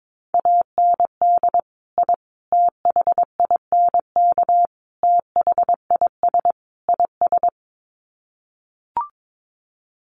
Morse code, heard as a keyed tone, about 22 wpm, 700 Hz. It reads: AND I THINK THIS IS
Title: Sets of 5 Words Encore 22wpm